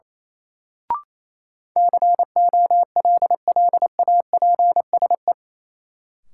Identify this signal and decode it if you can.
Morse code, heard as a keyed tone, about 28 wpm, 700 Hz. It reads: COLLAPSE
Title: Sets of 5 Words Encore - New Word Drill - 28wpm